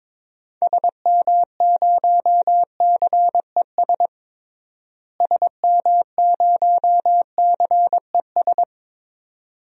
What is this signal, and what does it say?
Morse code, heard as a keyed tone, about 21 wpm, 700 Hz. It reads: SM0CES SM0CES
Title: DXCC-100 Call Signs - 6 Characters - Repeated 2x - 22wpm